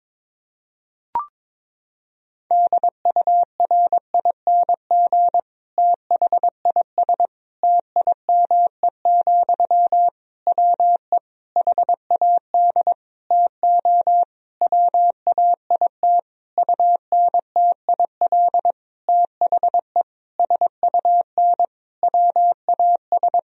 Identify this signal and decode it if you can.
Morse code, heard as a keyed tone, about 22 wpm, 700 Hz. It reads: DURING THIS TIME, WE HAD TO WAIT UNTIL THE SUN WAS
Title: Sentences from Top 1000 Words - Review Part 2 - 22wpm